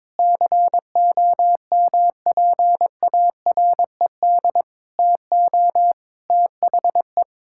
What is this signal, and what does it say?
Morse code, heard as a keyed tone, about 21 wpm, 700 Hz. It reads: COMPARED TO THE
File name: practice / 22wpm / Sentences with 8-Word Phrases - 22wpm